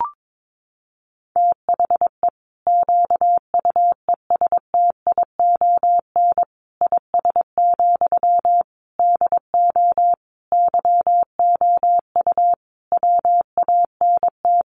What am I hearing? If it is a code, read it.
Morse code, heard as a keyed tone, about 22 wpm, 700 Hz. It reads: THE QUESTION IS, DO YOU WANT
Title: Sentences from Top 1000 Words 22wpm